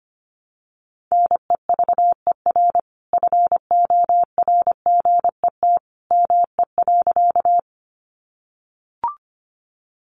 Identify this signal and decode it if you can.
Morse code, heard as a keyed tone, about 25 wpm, 700 Hz. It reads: NEVER FORGET ME.
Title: Sentences from Top 1200 Words 25wpm